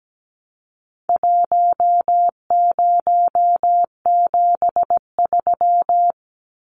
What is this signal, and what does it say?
Morse code, heard as a keyed tone, about 17 wpm, 700 Hz. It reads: 1073
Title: Numbers - 4-Digit 17wpm